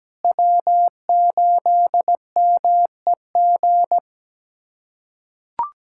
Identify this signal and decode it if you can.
Morse code, heard as a keyed tone, about 17 wpm, 700 Hz. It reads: W8MEG
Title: USA Callsign DE Callsign - 5 Characters - 17wpm